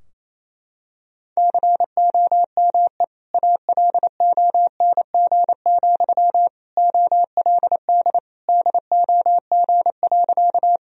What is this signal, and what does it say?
Morse code, heard as a keyed tone, about 28 wpm, 700 Hz. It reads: COME ALONG, OLD DOG.
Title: Sentences from Top 1000 Words - Review Part 2 - 28wpm